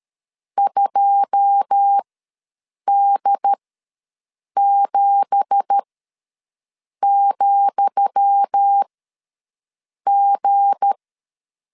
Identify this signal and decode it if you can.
Morse code, heard as a keyed tone, about 13 wpm, spaced as for 6 wpm, 800 Hz. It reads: 2D7,G
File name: LEZIONI / 10 SIGNS / 10D COMA ws 12wpm 3618